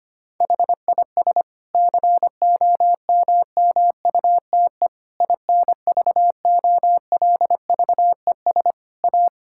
Morse code, heard as a keyed tone, about 25 wpm, 700 Hz. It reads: HIS COMMUTE INVOLVES A
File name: practice / 25wpm / Difficult Sentences with Hints - 25wpm